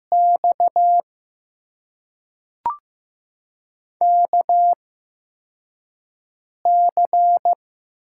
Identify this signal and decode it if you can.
Morse code, heard as a keyed tone, about 15 wpm, spaced as for 4 wpm, 700 Hz. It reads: X KC
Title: USA Call Signs - 6 Characters - 8x Character Spacing - 15wpm